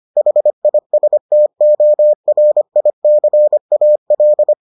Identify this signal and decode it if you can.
Morse code, heard as a keyed tone, about 25 wpm, 600 Hz. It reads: HISTORICAL